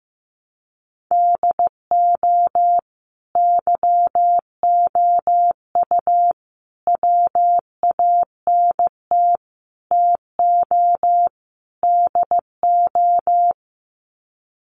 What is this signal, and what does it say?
Morse code, heard as a keyed tone, about 15 wpm, 700 Hz. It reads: DO YOU WANT TO DO
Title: Sets of 5 Words Encore 15wpm